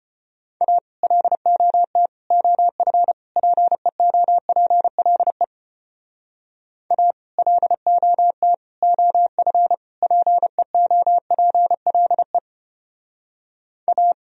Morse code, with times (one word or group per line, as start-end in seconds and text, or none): 0.6-0.8 A
1.0-2.1 LOT
2.3-3.1 OF
3.4-5.4 PEOPLE
6.9-7.1 A
7.4-8.5 LOT
8.8-9.7 OF
10.0-12.4 PEOPLE
13.9-14.1 A